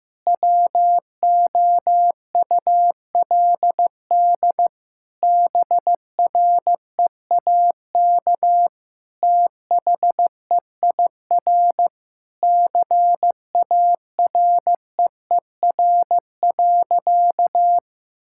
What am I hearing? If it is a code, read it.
Morse code, heard as a keyed tone, about 15 wpm, 700 Hz. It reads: WOULD BREAK THEIR CAREER.
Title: Sentences from Top 1000 Words - Review Part 6 - 15wpm